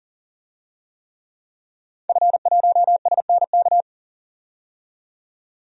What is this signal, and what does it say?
Morse code, heard as a keyed tone, about 40 wpm, 700 Hz. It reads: F1SNK
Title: DXCC-100 Call Signs - 5 Characters - 40wpm